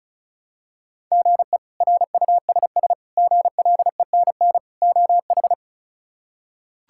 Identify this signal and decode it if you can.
Morse code, heard as a keyed tone, about 34 wpm, 700 Hz. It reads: GE RUSS GLENN OH
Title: K1USN SST QSO Practice - 35wpm